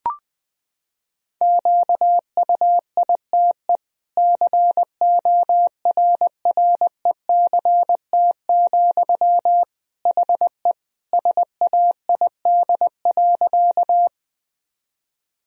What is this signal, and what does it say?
Morse code, heard as a keyed tone, about 21 wpm, 700 Hz. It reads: QUITE CORRECT, HE SAID.
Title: Sentences from Top 1000 Words - Review Part 2 - 20wpm